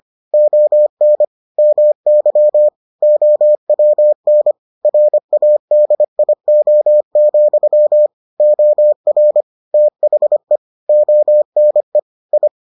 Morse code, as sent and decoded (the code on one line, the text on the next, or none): --- -. / -- -.-- / --- .-- -. / .-. .- -.. .. --- --..-- / --- .-. / - .... / --- -. / ..
ON MY OWN RADIO, OR THE ONE I